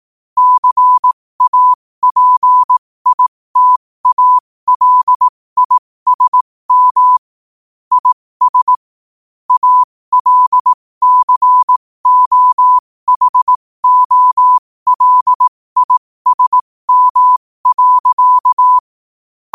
Morse code, heard as a keyed tone, about 18 wpm, spaced as for 15 wpm, 1000 Hz. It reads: CAPITALISM IS ALCOHOLISM.
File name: qotdfiles / 15wpm